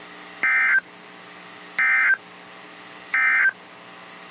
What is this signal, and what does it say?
Emergency Alert System (SAME) tones, heard as harsh buzzing bursts. It is heard from 0.3 to 3.6 s.
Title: NOAA Weather Radio - KEC55